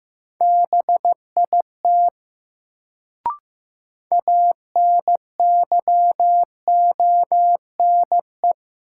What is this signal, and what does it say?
Morse code, heard as a keyed tone, about 15 wpm, 700 Hz. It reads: BIT ANYONE
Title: Sentences from Top 900 Words - New Word Drill - 15wpm